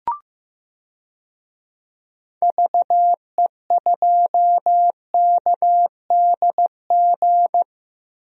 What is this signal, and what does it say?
Morse code, heard as a keyed tone, about 15 wpm, 700 Hz. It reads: VE2KDG
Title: DXCC-100 Call Signs - 6 Characters - 15wpm